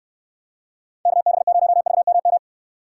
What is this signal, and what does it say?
Morse code, heard as a keyed tone, about 66 wpm, 700 Hz. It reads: KB0VMM